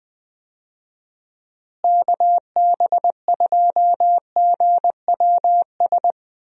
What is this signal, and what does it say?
Morse code, heard as a keyed tone, about 20 wpm, 700 Hz. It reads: KB2GWS
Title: USA Call Signs - 6 Characters - Repeated 2x - 20wpm